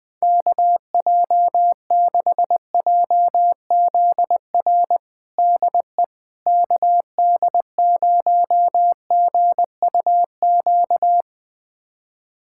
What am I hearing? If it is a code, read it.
Morse code, heard as a keyed tone, about 20 wpm, 700 Hz. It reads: KJ6JZR DE KD0GUQ
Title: USA Callsign DE Callsign 20wpm